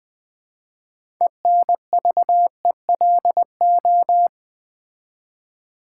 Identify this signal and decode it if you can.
Morse code, heard as a keyed tone, about 20 wpm, 700 Hz. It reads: ENVELO
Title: Build A Word - 3 Letters At A Time - 8-Letter Words - 20wpm